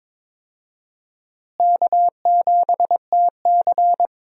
Morse code, heard as a keyed tone, about 22 wpm, 700 Hz. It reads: K7TC
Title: USA Call Signs - 4 Characters - Repeated 2x - 22wpm